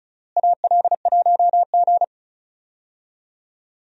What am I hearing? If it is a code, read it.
Morse code, heard as a keyed tone, about 36 wpm, 700 Hz. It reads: AL1G
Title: USA Call Signs - 4 Characters - Repeated 2x - 35wpm